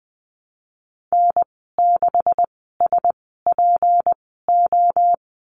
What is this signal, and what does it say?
Morse code, heard as a keyed tone, about 20 wpm, spaced as for 14 wpm, 700 Hz. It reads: N6SPO
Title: 161 Lesson 26g - Letter g - Call Signs 20wpm